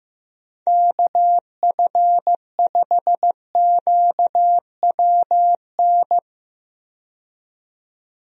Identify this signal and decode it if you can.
Morse code, heard as a keyed tone, about 15 wpm, 700 Hz. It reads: KF5QWN
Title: USA Call Signs - 6 Characters - 15wpm